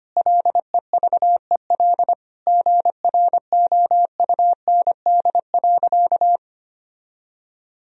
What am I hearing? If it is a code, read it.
Morse code, heard as a keyed tone, about 25 wpm, 700 Hz. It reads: LEVEL GROUND.
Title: Sentences from Top 1000 Words - Review Part 4 - 25wpm